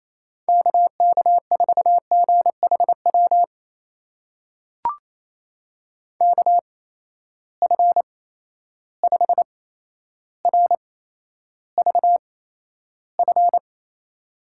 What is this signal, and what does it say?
Morse code, heard as a keyed tone, about 28 wpm, spaced as for 12 wpm, 700 Hz. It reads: KK4GHW KF5RVF
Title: USA Call Signs - 6 Characters - 8x Character Spacing - 28wpm